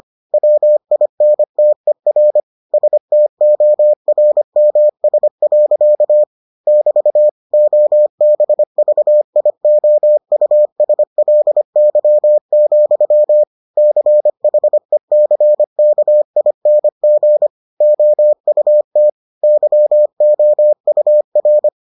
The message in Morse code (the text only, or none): WINTER STORMS. <BT> OBVIOUSLY, CHECKING OUT YOUR